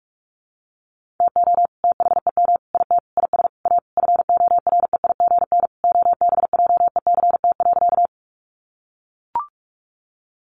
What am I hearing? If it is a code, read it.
Morse code, heard as a keyed tone, about 44 wpm, 700 Hz. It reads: TO THEM IT IS A FOREIGN OBJECT.
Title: Sentences from Top 1400 Words 45wpm